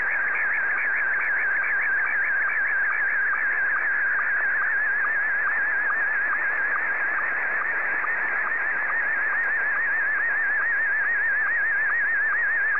SSTV (slow-scan television), Scottie 1.